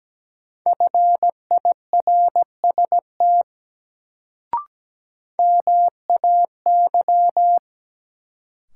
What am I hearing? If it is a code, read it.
Morse code, heard as a keyed tone, about 17 wpm, 700 Hz. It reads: FIRST MAY
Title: Sentences from Top 100 Words - New Word Drill - 17wpm